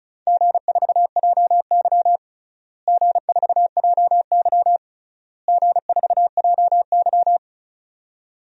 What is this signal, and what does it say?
Morse code, heard as a keyed tone, about 34 wpm, 700 Hz. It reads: G4JY G4JY G4JY
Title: DXCC-100 Call Signs - 4 Characters - Repeated 3x - 35wpm